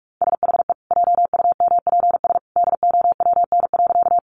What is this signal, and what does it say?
Morse code, heard as a keyed tone, about 44 wpm, 700 Hz. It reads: SHE JUMPS DOWN.
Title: Easy Phrases 45wpm